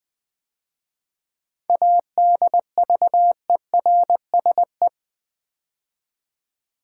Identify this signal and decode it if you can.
Morse code, heard as a keyed tone, about 20 wpm, 700 Hz. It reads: ADVERSE
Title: 7 Letter Words - First 3 Letters Given - 20wpm